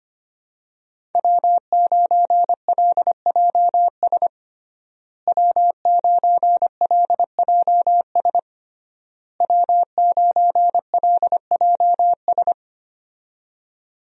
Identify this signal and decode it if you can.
Morse code, heard as a keyed tone, about 25 wpm, 700 Hz. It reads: W9LJS W9LJS W9LJS
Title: USA Call Signs - 5 Characters - Repeated 3x - 25wpm